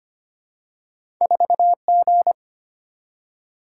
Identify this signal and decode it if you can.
Morse code, heard as a keyed tone, about 25 wpm, 700 Hz. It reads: 4G
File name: Two Letter-Number Combinations - 25wpm